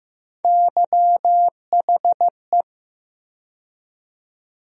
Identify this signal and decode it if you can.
Morse code, heard as a keyed tone, about 15 wpm, 700 Hz. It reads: YHE
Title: USA Call Sign Postfixes - 3 Letters 15wpm